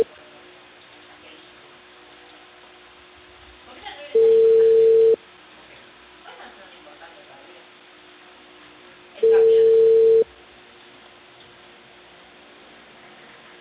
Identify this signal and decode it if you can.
Ringback tone (China, 450 Hz), with speech.